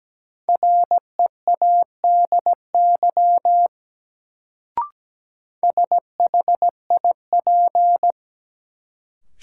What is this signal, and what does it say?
Morse code, heard as a keyed tone, about 17 wpm, 700 Hz. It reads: READY SHIP
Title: Sentences from Top 400 Words - New Word Drill - 17wpm